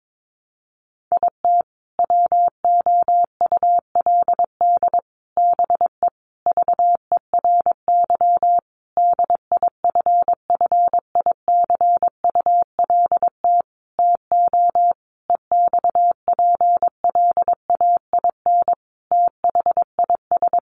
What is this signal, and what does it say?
Morse code, heard as a keyed tone, about 22 wpm, 700 Hz. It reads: IT WOULD BE VERY DIFFICULT TO EXPLAIN THIS
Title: Sentences from Top 1100 Words 22wpm